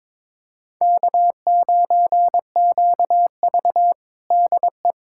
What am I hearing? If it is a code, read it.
Morse code, heard as a keyed tone, about 22 wpm, 700 Hz. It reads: K9QV DE
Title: USA Callsign DE Callsign - 4 Characters - 22wpm